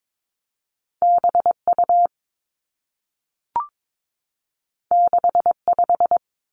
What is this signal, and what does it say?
Morse code, heard as a keyed tone, about 22 wpm, 700 Hz. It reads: B U 6 5
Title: Two Challenging Letter-Number Combinations 22wpm